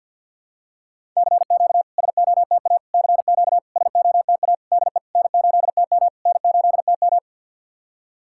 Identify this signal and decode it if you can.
Morse code, heard as a keyed tone, about 49 wpm, 700 Hz. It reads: CQ SOTA CQ SOTA DE N8TM N8TM
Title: SOTA QSO Practice 50wpm